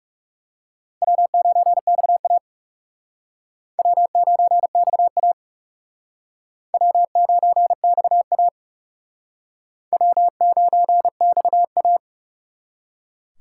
Morse code, heard as a keyed tone, about 36 wpm, 700 Hz. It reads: W9XA W9XA W9XA W9XA
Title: Speed Racing - USA 1x2 Call Signs 30wpm